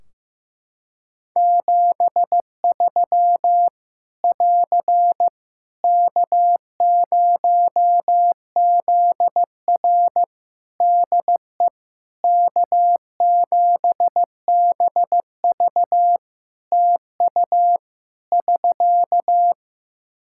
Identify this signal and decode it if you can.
Morse code, heard as a keyed tone, about 15 wpm, 700 Hz. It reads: 73 <AR> K0ZR DE K7BV TU <SK>
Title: K1USN SST QSO Practice 15wpm